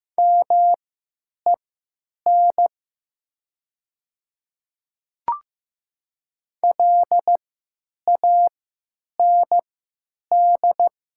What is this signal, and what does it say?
Morse code, heard as a keyed tone, about 15 wpm, spaced as for 9 wpm, 700 Hz. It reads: MEN LAND
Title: 3x Character Spacing - Top 300 Words - 15wpm